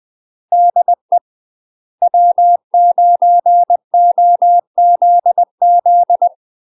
Morse code, heard as a keyed tone, about 20 wpm, 700 Hz. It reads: DE W9OZZ
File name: UK MASTER QSOs TRAINING / 20WPM / UK MASTER QSOs 0043